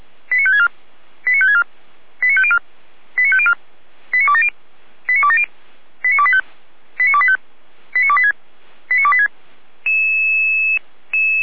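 Five-tone selective call (ZVEI), heard as quick runs of five tones, tones of about 70 ms. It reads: 89564 89564 89504 89504 89270 89270 89285 89285 89286 89286
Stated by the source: ZVEI five-tone selective call received by SDR.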